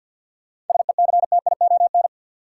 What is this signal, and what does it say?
Morse code, heard as a keyed tone, about 52 wpm, 700 Hz. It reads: SECTION
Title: Sentences from Top 1000 Words - Review Part 5 - New Word Drill - 50wpm